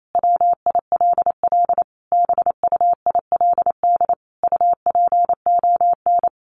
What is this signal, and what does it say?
Morse code, heard as a keyed tone, about 28 wpm, 700 Hz. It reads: WILL BUILD UPON
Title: Sentences from Top 1400 Words 28wpm